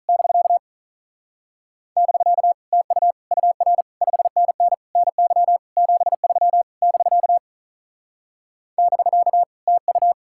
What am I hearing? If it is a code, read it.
Morse code, heard as a keyed tone, about 44 wpm, 700 Hz. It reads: <BK> <BK> TU UR 5NN NY 73 <BK> <BK> TU
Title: Speed-Racing Common QSO Phrases Encore - 35wpm